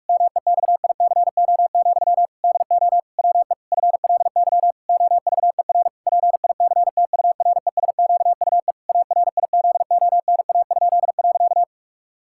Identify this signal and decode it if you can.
Morse code, heard as a keyed tone, about 41 wpm, 700 Hz. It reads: MEXICO, DO WE FLY OVER PICTURESQUE ARIZONA?.
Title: Pangram Challenge v2 - 45wpm